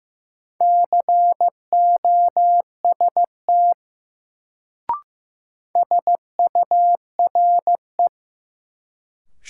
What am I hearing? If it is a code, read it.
Morse code, heard as a keyed tone, about 16 wpm, 700 Hz. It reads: COST SURE
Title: Sets of 3 Words - New Word Drill - 15wpm